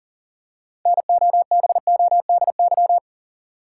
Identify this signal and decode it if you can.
Morse code, heard as a keyed tone, about 40 wpm, 700 Hz. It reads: NOBODY
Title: Six Letter Words - 40wpm